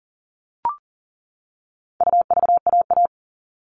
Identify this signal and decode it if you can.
Morse code, heard as a keyed tone, about 40 wpm, 700 Hz. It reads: UVAA
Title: All Cut Numbers - 4-Digit 40wpm